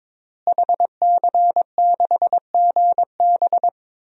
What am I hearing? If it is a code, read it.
Morse code, heard as a keyed tone, about 22 wpm, 700 Hz. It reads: HC6GB
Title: DXCC-100 Call Signs - 5 Characters - Repeated 2x - 22wpm